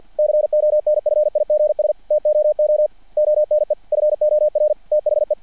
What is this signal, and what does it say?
Morse code, heard as a keyed tone, about 50 wpm, 600 Hz. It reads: CONFIGS TOO ONE FOR THE
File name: archived / CRSnet